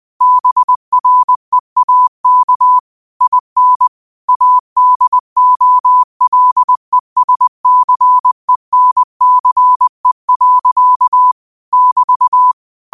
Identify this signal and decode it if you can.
Morse code, heard as a keyed tone, about 20 wpm, 1000 Hz. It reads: BREAK IN ADOLESCENCE. <BT>
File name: qotdfiles / 20wpm